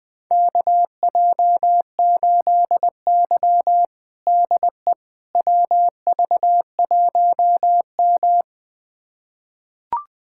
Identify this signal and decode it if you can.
Morse code, heard as a keyed tone, about 21 wpm, 700 Hz. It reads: KJ8Y DE WV1M
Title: USA Callsign DE Callsign - 4 Characters - 20wpm